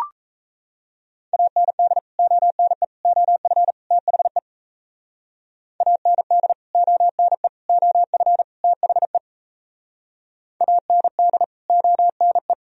Morse code, heard as a keyed tone, about 36 wpm, 700 Hz. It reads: AND ONE OF THE AND ONE OF THE AND ONE